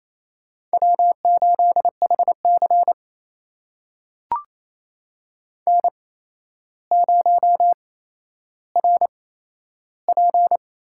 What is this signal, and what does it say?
Morse code, heard as a keyed tone, about 28 wpm, spaced as for 12 wpm, 700 Hz. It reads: W8HC N0RP
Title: USA Call Signs - 4 Characters - 8x Character Spacing - 28wpm